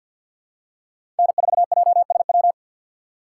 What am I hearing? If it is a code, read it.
Morse code, heard as a keyed tone, about 52 wpm, 700 Hz. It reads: N4JSW
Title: USA Call Signs - 2x Character Spacing - 50wpm